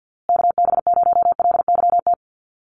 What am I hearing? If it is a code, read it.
Morse code, heard as a keyed tone, about 50 wpm, 700 Hz. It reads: KB0LYT
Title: USA Call Signs Repeated 3x 50wpm